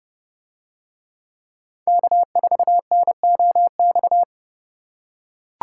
Morse code, heard as a keyed tone, about 29 wpm, 700 Hz. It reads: K4NOX